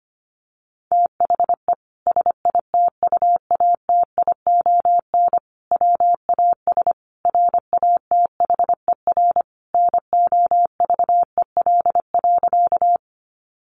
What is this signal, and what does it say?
Morse code, heard as a keyed tone, about 25 wpm, 700 Hz. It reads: THE SITUATION WAS RATHER NOVEL.
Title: Sentences from Top 1300 Words 25wpm